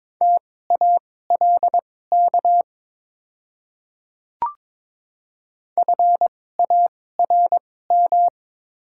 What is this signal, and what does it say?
Morse code, heard as a keyed tone, about 22 wpm, spaced as for 16 wpm, 700 Hz. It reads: TALK FARM
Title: 2x Character Spacing - Top 500 Words - 22wpm